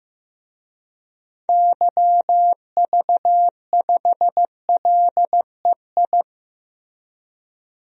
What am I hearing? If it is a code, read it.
Morse code, heard as a keyed tone, about 15 wpm, 700 Hz. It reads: YV5LEI
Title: DXCC-100 Call Signs - 6 Characters - 15wpm